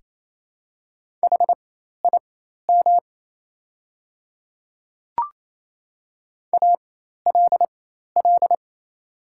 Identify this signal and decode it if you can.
Morse code, heard as a keyed tone, about 27 wpm, spaced as for 13 wpm, 700 Hz. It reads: HIM ALL
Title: 4x Character Spacing - Top 100 Words - 28wpm